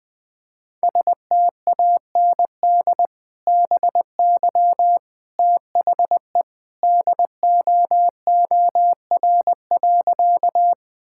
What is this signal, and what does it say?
Morse code, heard as a keyed tone, about 20 wpm, 700 Hz. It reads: STAND BY THE DOOR.
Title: Sentences with 2-Word Phrases - 20wpm